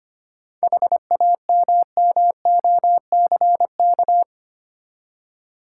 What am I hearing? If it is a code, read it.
Morse code, heard as a keyed tone, about 25 wpm, 690 Hz. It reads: HAMMOCK
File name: practice / 25wpm / Easy Words with Hints - 25wpm